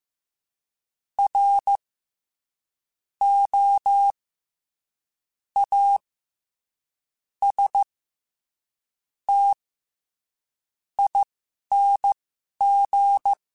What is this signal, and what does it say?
Morse code, heard as a keyed tone, about 15 wpm, spaced as for 6 wpm, 800 Hz. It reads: ROASTING
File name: Special INGs-mp3s / Special INGs-mp3s / SlowToFaster ing @10